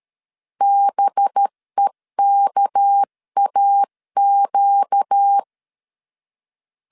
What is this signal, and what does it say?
Morse code, heard as a keyed tone, about 13 wpm, 800 Hz. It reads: BEKAQ
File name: CW / DET / 1 6 1500C S C 12wpm 3533